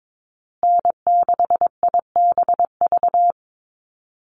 Morse code, heard as a keyed tone, about 22 wpm, 700 Hz. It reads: N6IBV
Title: USA Call Signs Repeated 2x 22wpm